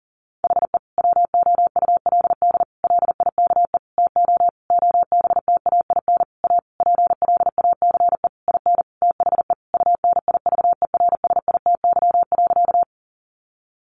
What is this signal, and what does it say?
Morse code, heard as a keyed tone, about 40 wpm, 700 Hz. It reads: HE WOULD LIKE TO OBTAIN A PLACE IN THE UNIVERSITY.
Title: Sentences from Top 1300 Words 40wpm